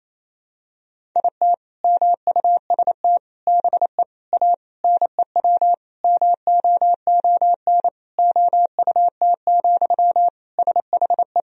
Morse code, heard as a keyed tone, about 27 wpm, 700 Hz. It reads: IT MUST BE A NEW MOON OUT, SHE
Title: Sentences from Top 500 Words and Abbreviations 28wpm